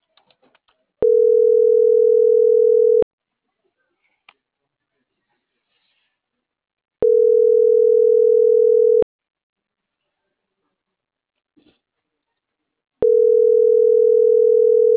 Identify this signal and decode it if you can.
Ringback tone (North America), 440 + 480 Hz.